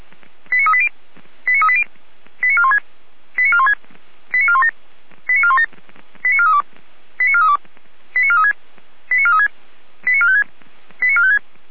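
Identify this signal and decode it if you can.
Five-tone selective call (ZVEI), heard as quick runs of five tones, tones of about 70 ms. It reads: 89380 89380 89416 89416 89417 89417 89432 89432 89436 89436 89456 89456
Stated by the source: Received on SDR, ZVEI five-tone selective call.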